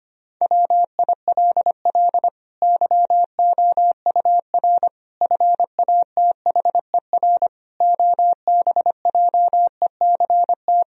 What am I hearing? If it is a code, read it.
Morse code, heard as a keyed tone, about 25 wpm, 700 Hz. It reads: WILL YOUR FATHER OBJECT
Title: Sentences from Top 1000 Words - Review Part 3 - 25wpm